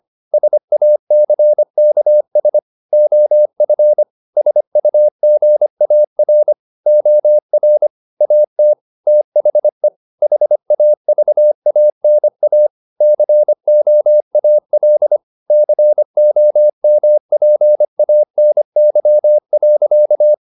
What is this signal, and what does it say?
Morse code, heard as a keyed tone, about 25 wpm, 600 Hz. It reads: SACKS OF SUGAR OR AT THE HAVANA COAL COMPANY.